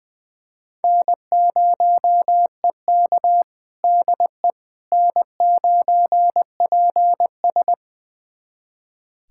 Morse code, heard as a keyed tone, about 20 wpm, 700 Hz. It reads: N0EK DE N9PS
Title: USA Callsign DE Callsign - 4 Characters - 20wpm